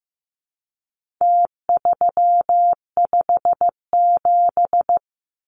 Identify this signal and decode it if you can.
Morse code, heard as a keyed tone, about 15 wpm, 700 Hz. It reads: T357
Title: Numbers - Contest Style - 4-Digit 15wpm